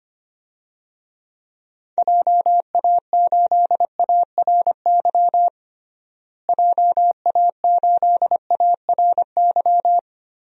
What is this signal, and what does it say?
Morse code, heard as a keyed tone, about 25 wpm, 700 Hz. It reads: JA8ARY JA8ARY
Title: DXCC-100 Call Signs - 6 Characters - Repeated 2x - 25wpm